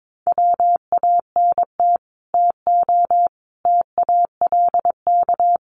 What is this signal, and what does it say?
Morse code, heard as a keyed tone, about 22 wpm, 700 Hz. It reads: WANT TO TALK
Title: Sets of 4 Words 22wpm